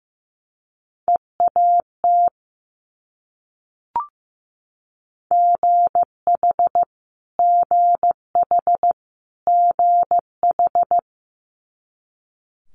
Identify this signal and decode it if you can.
Morse code, heard as a keyed tone, about 15 wpm, 700 Hz. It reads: EAT GH GH GH
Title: Multi-letter Phonemes Repeated 3x 15wpm